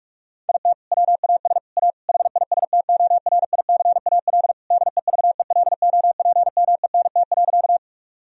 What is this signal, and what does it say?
Morse code, heard as a keyed tone, about 45 wpm, 700 Hz. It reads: IT WAS A HISTORICAL DEVELOPMENT.
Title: Sentences from Top 1000 Words - Review Part 4 - 45wpm